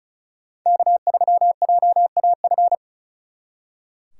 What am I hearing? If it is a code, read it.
Morse code, heard as a keyed tone, about 46 wpm, 700 Hz. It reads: K3JAF